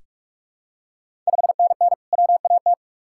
Morse code, heard as a keyed tone, about 46 wpm, 700 Hz. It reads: 5NN WAT
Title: Ontario QSO Party Exchange 45wpm